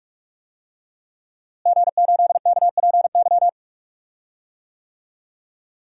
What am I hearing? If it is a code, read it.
Morse code, heard as a keyed tone, about 44 wpm, 700 Hz. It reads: G8KPY
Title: DXCC-100 Call Signs - 5 Characters - Repeated 2x - 45wpm